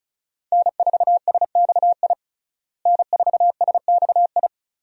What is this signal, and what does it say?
Morse code, heard as a keyed tone, about 36 wpm, 700 Hz. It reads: N4SXI N4SXI
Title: USA Call Signs - 5 Characters - Repeated 3x - 35wpm